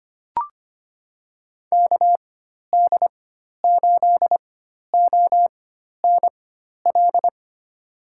Morse code, heard as a keyed tone, about 25 wpm, spaced as for 12 wpm, 700 Hz. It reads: KD8ONL
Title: USA Call Signs - 4x Character Spacing - 25wpm